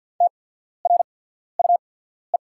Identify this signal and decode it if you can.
Morse code, heard as a keyed tone, about 48 wpm, spaced as for 14 wpm, 690 Hz. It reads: TRUE
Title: 8x Character Spacing - Top 500 Words - 50wpm